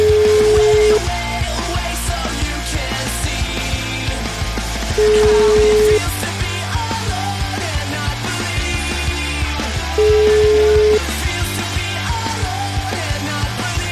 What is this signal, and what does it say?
Ringback tone (Europe, 425 Hz), with noise.